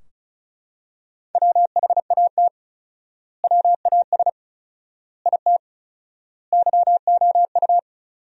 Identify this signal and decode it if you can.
Morse code, heard as a keyed tone, about 34 wpm, 700 Hz. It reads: WHAT WAS IT YOU
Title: 4x Word-Spacing - Sentences from Top 200 Words - 35wpm